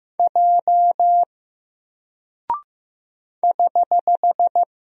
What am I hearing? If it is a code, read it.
Morse code, heard as a keyed tone, about 15 wpm, 700 Hz. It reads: J <HH>
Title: Single Character (Letter:Number:Punctuation:Prosign) 15wpm